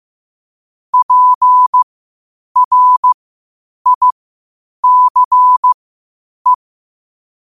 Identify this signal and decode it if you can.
Morse code, heard as a keyed tone, about 15 wpm, spaced as for 8 wpm, 1000 Hz. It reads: PRICE
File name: qotdfiles / 8wpm